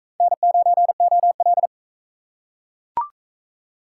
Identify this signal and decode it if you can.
Morse code, heard as a keyed tone, about 39 wpm, 700 Hz. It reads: N9OL